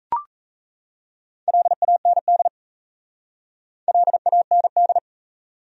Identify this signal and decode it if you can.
Morse code, heard as a keyed tone, about 37 wpm, 700 Hz. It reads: LAND LAND